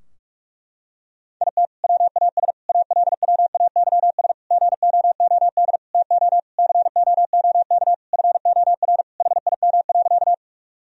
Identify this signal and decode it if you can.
Morse code, heard as a keyed tone, about 46 wpm, 700 Hz. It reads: IT WAS ALWAYS GOOD TO COOK FOR HIM.
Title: Sentences from Top 1000 Words - Review Part 4 - 45wpm